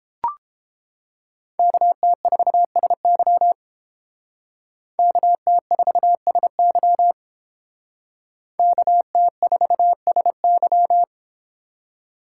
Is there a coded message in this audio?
Yes: Morse code from 1.6 to 11.0 s, heard as a keyed tone, about 30 wpm, 700 Hz. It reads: KT4SY KT4SY KT4SY